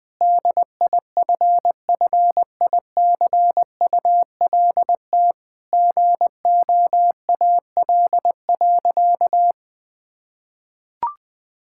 Morse code, heard as a keyed tone, about 20 wpm, 700 Hz. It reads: DIFFICULT GOAL.
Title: Sentences from Top 1000 Words - Review Part 5 - 20wpm